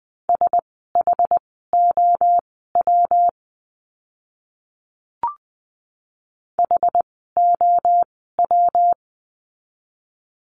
Morse code, heard as a keyed tone, about 20 wpm, spaced as for 14 wpm, 700 Hz. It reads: SHOW HOW
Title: 206 Lesson 33f - Letter j - Words 20wpm